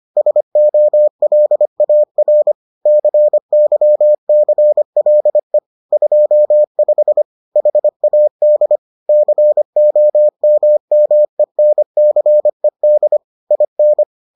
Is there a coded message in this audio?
Yes: Morse code from 0.2 to 14.0 s, heard as a keyed tone, about 25 wpm, 600 Hz. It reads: SOLAR CYCLE 25 HAD COMMENCED IN